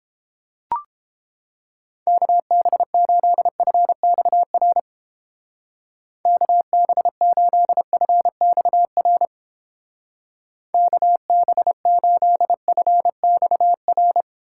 Morse code, heard as a keyed tone, about 29 wpm, 700 Hz. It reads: KB8FXR KB8FXR KB8FXR